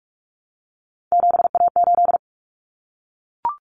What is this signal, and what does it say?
Morse code, heard as a keyed tone, about 44 wpm, 700 Hz. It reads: 7A8
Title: Common Cut Numbers - 3-Digit 45wpm